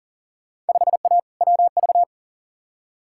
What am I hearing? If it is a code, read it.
Morse code, heard as a keyed tone, about 40 wpm, 700 Hz. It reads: 5A WV
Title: Field Day Exchanges - 40wpm